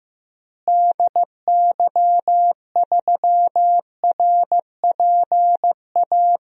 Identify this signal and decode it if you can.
Morse code, heard as a keyed tone, about 16 wpm, 700 Hz. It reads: DY3RPA
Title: DXCC-100 Call Signs - 6 Characters - Repeated 2x - 15wpm